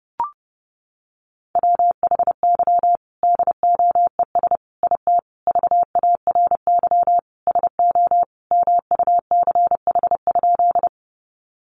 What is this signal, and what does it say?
Morse code, heard as a keyed tone, about 30 wpm, 700 Hz. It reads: WHY DOES IT VARY SO MUCH?
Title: Sentences from Top 1200 Words 30wpm